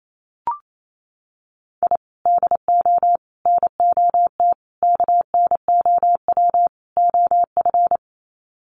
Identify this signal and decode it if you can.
Morse code, heard as a keyed tone, about 28 wpm, 700 Hz. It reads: I DO NOT KNOW OF
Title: Sets of 5 Words Encore2 28wpm